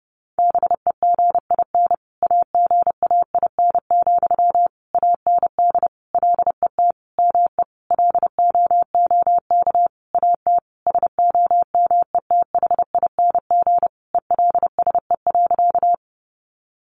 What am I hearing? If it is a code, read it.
Morse code, heard as a keyed tone, about 30 wpm, 700 Hz. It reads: BEGIN AGAIN, AND LET ME LOOK AT SOMETHING ELSE.
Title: Sentences from Top 800 Words 30wpm